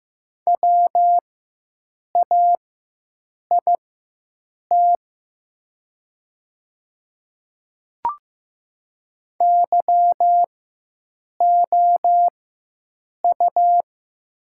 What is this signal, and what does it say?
Morse code, heard as a keyed tone, about 15 wpm, spaced as for 7 wpm, 700 Hz. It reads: WAIT YOU
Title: 4x Character Spacing - Top 500 Words - 15wpm